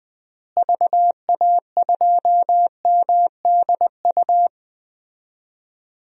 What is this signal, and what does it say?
Morse code, heard as a keyed tone, about 19 wpm, 700 Hz. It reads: VA2MDU
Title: DXCC-100 Call Signs - 6 Characters - Repeated 3x - 20wpm